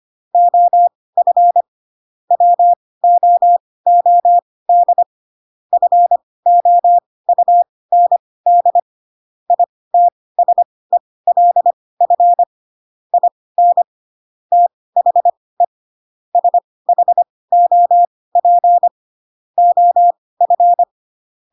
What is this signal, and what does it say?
Morse code, heard as a keyed tone, about 25 wpm, spaced as for 18 wpm, 700 Hz. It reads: OF WOOD FOUND ITSELF IN THE SHOP OF
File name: Pinocchio / Pinocchio ch1 12 25 ik1hge